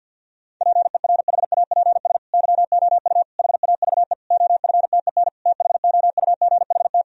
Morse code, heard as a keyed tone, about 52 wpm, 700 Hz. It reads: PERHAPS YOU HAVE OFTEN THOUGHT
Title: Sentences from Top 1000 Words - Review Part 1 - 50wpm